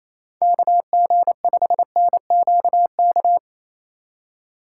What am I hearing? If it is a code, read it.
Morse code, heard as a keyed tone, about 28 wpm, 700 Hz. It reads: KG5NQK
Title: USA Call Signs - 6x Character Spacing - 28wpm